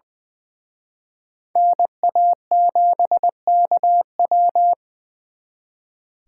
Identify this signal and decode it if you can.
Morse code, heard as a keyed tone, about 23 wpm, 700 Hz. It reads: NA7KW
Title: Speed Racing - USA 2x2 Call Signs 20wpm